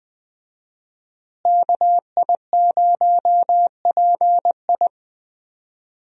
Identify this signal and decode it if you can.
Morse code, heard as a keyed tone, about 20 wpm, 700 Hz. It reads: KI0PI
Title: USA Call Signs - 5 Characters - Repeated 2x - 20wpm